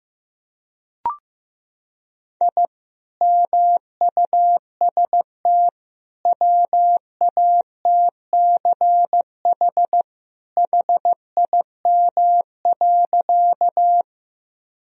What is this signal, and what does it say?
Morse code, heard as a keyed tone, about 15 wpm, 700 Hz. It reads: I MUST WATCH HIM.
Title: Sentences from Top 1000 Words - Review Part 1 - 15wpm